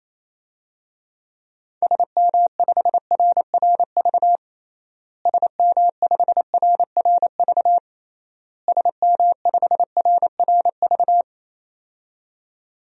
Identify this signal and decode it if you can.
Morse code, heard as a keyed tone, about 28 wpm, 700 Hz. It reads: SM5RRV SM5RRV SM5RRV
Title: DXCC-100 Call Signs - 6 Characters - Repeated 3x - 28wpm